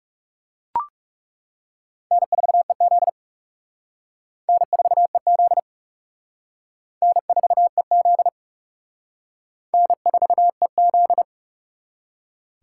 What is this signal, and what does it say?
Morse code, heard as a keyed tone, about 40 wpm, 700 Hz. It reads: N4EZ N4EZ N4EZ N4EZ